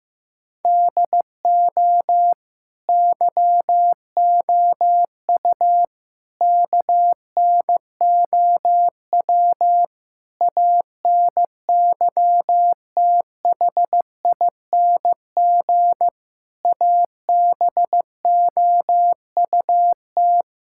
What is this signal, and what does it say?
Morse code, heard as a keyed tone, about 15 wpm, 700 Hz. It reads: DO YOU KNOW ANYTHING ABOUT
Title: Sentences from Top 1600 Words 15wpm